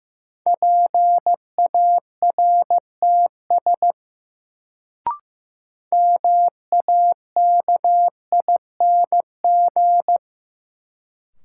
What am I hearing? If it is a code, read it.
Morse code, heard as a keyed tone, about 16 wpm, 700 Hz. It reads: PARTS MAKING
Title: Sentences from Top 1000 Words - Review Part 6 - New Word Drill - 15wpm